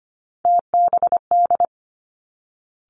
Morse code, heard as a keyed tone, about 25 wpm, 700 Hz. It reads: TBD
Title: Three Challenging Letter-Number Combinations 25wpm